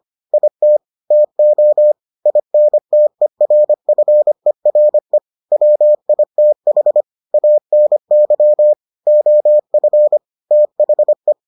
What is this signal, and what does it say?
Morse code, heard as a keyed tone, about 25 wpm, 600 Hz. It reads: IT TO INTERFERE WITH ANY OF THE